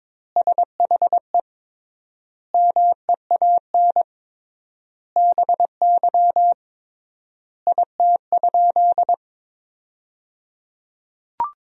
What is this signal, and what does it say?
Morse code, heard as a keyed tone, about 22 wpm, 700 Hz. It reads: SHE MEAN BY IT?
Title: 3x Word-Spacing - Sentences from Top 200 Words - 22wpm